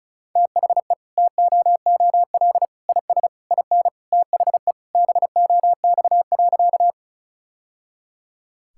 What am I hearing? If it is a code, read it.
Morse code, heard as a keyed tone, about 34 wpm, 700 Hz. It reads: THE TOOL IS IN THE BOX.
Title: Sentences from Top 1000 Words - Review Part 6 - 35wpm